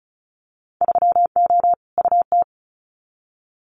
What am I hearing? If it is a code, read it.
Morse code, heard as a keyed tone, about 36 wpm, 700 Hz. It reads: 3O UT
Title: Winter Field Day Exchanges 35wpm